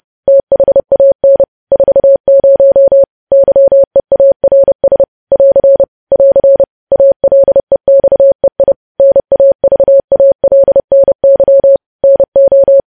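Morse code, heard as a keyed tone, about 30 wpm, 550 Hz. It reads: THAN 40 YEARS <AR> <AR> ALEXEI NAVALNY NO
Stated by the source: Morse code sending of news headlines at 30 wpm.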